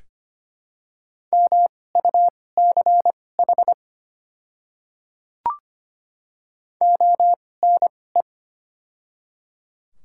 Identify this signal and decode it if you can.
Morse code, heard as a keyed tone, about 25 wpm, spaced as for 18 wpm, 700 Hz. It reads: MUCH ONE
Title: 2x Character Spacing - Top 200 Words - 25wpm